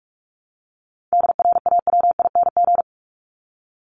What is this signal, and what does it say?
Morse code, heard as a keyed tone, about 44 wpm, 700 Hz. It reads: DRAWING
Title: Top 1751-2000 Words 45wpm